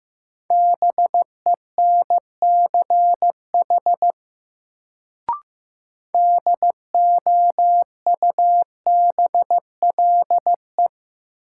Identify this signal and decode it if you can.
Morse code, heard as a keyed tone, about 15 wpm, 700 Hz. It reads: BENCH DOUBLE
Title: Sentences from Top 1400 Words - New Word Drill Alt - 15wpm